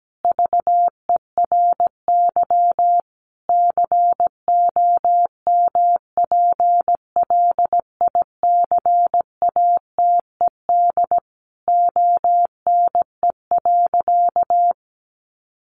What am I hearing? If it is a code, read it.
Morse code, heard as a keyed tone, about 17 wpm, 700 Hz. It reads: VERY COMPLICATED ONE.
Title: Sentences from Top 1200 Words 17wpm